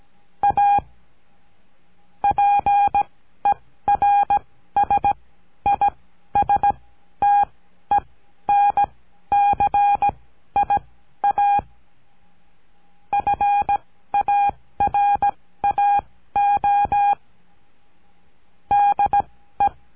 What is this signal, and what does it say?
Morse code, heard as a keyed tone, about 17 wpm, spaced as for 12 wpm, 810 Hz. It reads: A PERSISTENCIA FARAO DE
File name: CURSO DE TELEGRAFIA